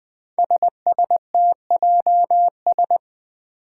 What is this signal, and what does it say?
Morse code, heard as a keyed tone, about 20 wpm, 700 Hz. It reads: SSTJS
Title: Five Challenging Letter-Number Combinations 20wpm